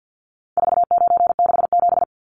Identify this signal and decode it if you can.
Morse code, heard as a keyed tone, about 50 wpm, 700 Hz. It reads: 4967
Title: Numbers - 4-Digit 50wpm